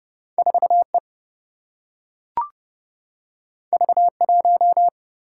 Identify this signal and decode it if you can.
Morse code, heard as a keyed tone, about 30 wpm, 700 Hz. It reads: 4E V1
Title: Single-Letter Single-Number 30wpm